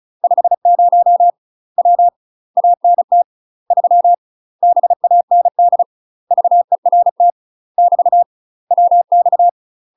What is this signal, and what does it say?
Morse code, heard as a keyed tone, about 35 wpm, 700 Hz. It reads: 50 W ANT 3 BAND VERT <BT> WX
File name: UK MASTER QSOs TRAINING / 35WPM / UK MASTER QSOs 0012